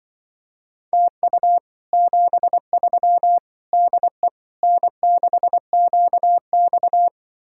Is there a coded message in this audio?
Yes: Morse code from 0.9 to 7.1 s, heard as a keyed tone, about 25 wpm, 700 Hz. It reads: TU 73 DE N6QX